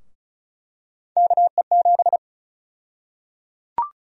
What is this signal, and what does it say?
Morse code, heard as a keyed tone, about 34 wpm, 700 Hz. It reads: KE7
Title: USA Call Sign Prefixes - 35wpm